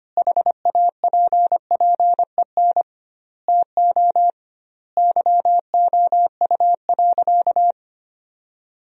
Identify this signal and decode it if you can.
Morse code, heard as a keyed tone, about 25 wpm, 700 Hz. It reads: HAPPEN TO YOU.
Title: 2x Word-Spacing - Sentences from Top 500 Words - 25wpm